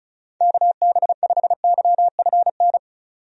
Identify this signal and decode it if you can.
Morse code, heard as a keyed tone, about 36 wpm, 700 Hz. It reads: KB5YFN
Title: USA Call Signs - 6 Characters - 35wpm